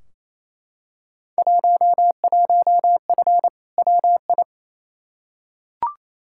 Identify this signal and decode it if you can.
Morse code, heard as a keyed tone, about 28 wpm, 700 Hz. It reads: 11F WI
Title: Field Day Exchanges Encore - 28wpm